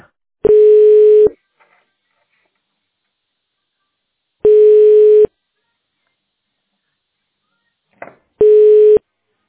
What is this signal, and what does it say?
Ringback tone (425 Hz, 1 s on, 3 s off).